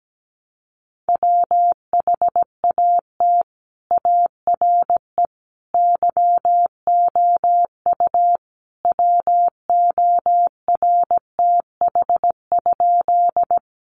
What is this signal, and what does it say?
Morse code, heard as a keyed tone, about 17 wpm, 700 Hz. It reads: WHAT ARE YOU WORTH?
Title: Sentences from Top 800 Words 17wpm